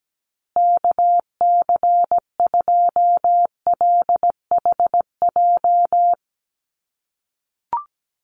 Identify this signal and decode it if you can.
Morse code, heard as a keyed tone, about 17 wpm, 700 Hz. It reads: KC2LHJ
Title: USA Call Signs Repeated 2x 17wpm